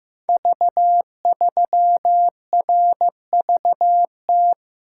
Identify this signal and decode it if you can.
Morse code, heard as a keyed tone, about 15 wpm, 700 Hz. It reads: V3RVT
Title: Code Groups - 5 Letter:Numbers - 15wpm